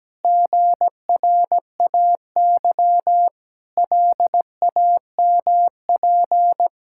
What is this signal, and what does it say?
Morse code, heard as a keyed tone, about 17 wpm, 700 Hz. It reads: GRAY LAMP
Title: Pangram Challenge v2 - 17wpm